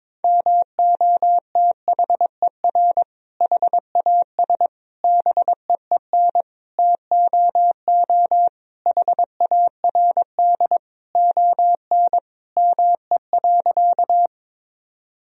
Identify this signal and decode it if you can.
Morse code, heard as a keyed tone, about 22 wpm, 700 Hz. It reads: MOTHER HAS BEEN TOO HARD ON ME.
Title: Sentences from Top 1000 Words - Review Part 1 - 22wpm